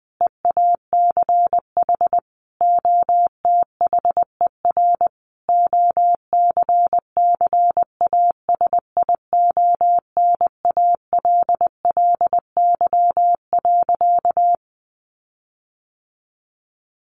Morse code, heard as a keyed tone, about 20 wpm, 700 Hz. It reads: EACH OTHER OCCASIONALLY.
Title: Sentences from Top 1300 Words 20wpm